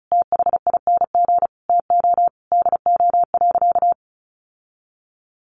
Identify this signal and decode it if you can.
Morse code, heard as a keyed tone, about 34 wpm, 700 Hz. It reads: THING TO DO.
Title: Sentences from Top 1200 Words 35wpm